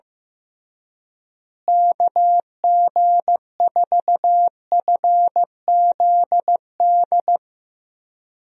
Morse code, heard as a keyed tone, about 15 wpm, 700 Hz. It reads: KG4FZD
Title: USA Call Signs - 6 Characters - Repeated 3x - 15wpm